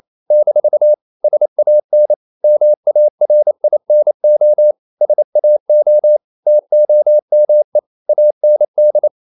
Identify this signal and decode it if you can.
Morse code, heard as a keyed tone, about 28 wpm, 600 Hz. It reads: - SAN MARINO SAO TOME AND
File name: CW / Practice / NationalCapitals28